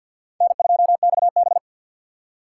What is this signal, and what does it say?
Morse code, heard as a keyed tone, about 49 wpm, 700 Hz. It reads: N2XB
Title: USA Call Signs - 4 Characters - 2x Character Spacing - 50wpm